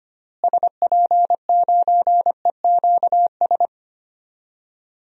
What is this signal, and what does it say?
Morse code, heard as a keyed tone, about 25 wpm, 700 Hz. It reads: SP9EQS
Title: DXCC-100 Call Signs - 6 Characters - Repeated 3x - 25wpm